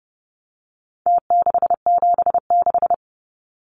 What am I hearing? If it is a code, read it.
Morse code, heard as a keyed tone, about 30 wpm, 700 Hz. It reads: T676
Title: Numbers - Contest Style - 4-Digit 30wpm